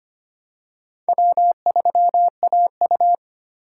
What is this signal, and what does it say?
Morse code, heard as a keyed tone, about 25 wpm, 700 Hz. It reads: W3AU
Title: USA Call Signs - 4 Characters - 4x Character Spacing - 25wpm